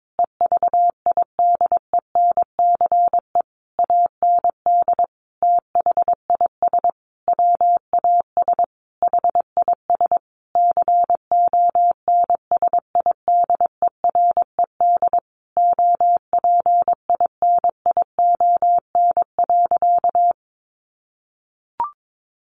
Morse code, heard as a keyed tone, about 22 wpm, 700 Hz. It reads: EVIDENCE AND THIS WAS HIS CONSIDERED OPINION.
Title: A Sound Of Thunder 22wpm